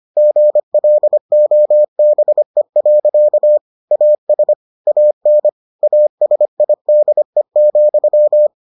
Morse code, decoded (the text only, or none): GLOBE. AS AN ASIDE,